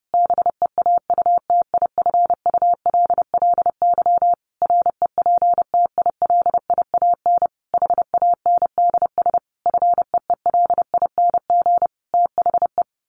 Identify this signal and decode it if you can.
Morse code, heard as a keyed tone, about 30 wpm, 700 Hz. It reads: BEAUTIFULLY REPTILIAN HANDS FEELING THE
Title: A Sound Of Thunder 30wpm